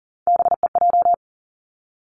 Morse code, heard as a keyed tone, about 42 wpm, 700 Hz. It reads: BEJ
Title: Three Challenging Letter-Number Combinations 40wpm